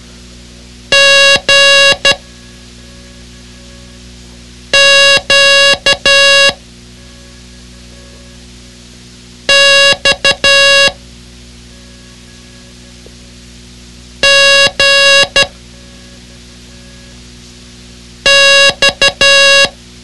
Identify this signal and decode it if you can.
Morse code, heard as a keyed tone, about 12 wpm, 1650 Hz. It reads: G Q X G X
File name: ham radio / CodeQuick / Code Quick 06